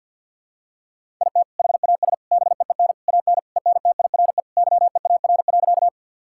Morse code, heard as a keyed tone, about 50 wpm, 700 Hz. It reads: IT HAS BEEN AN ENTIRE YEAR.
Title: Sentences from Top 1000 Words - Review Part 5 - 50wpm